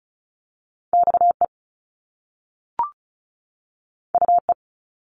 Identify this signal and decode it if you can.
Morse code, heard as a keyed tone, about 36 wpm, 700 Hz. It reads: XE UE
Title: Top Two Letter Combinations in Words 35wpm